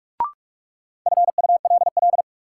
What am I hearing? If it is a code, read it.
Morse code, heard as a keyed tone, about 46 wpm, 700 Hz. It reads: FULL
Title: Sentences from Top 500 Words - New Word Drill - 45wpm